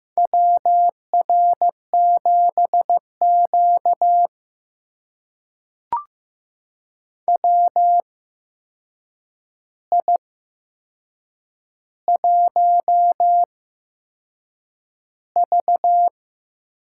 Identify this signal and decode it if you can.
Morse code, heard as a keyed tone, about 15 wpm, spaced as for 6 wpm, 700 Hz. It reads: WR7Q WI1V
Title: USA Call Signs - 4 Characters - 8x Character Spacing - 15wpm